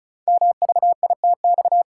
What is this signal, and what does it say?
Morse code, heard as a keyed tone, about 36 wpm, 700 Hz. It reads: MVITX
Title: Code Groups - 5 Letters - 35wpm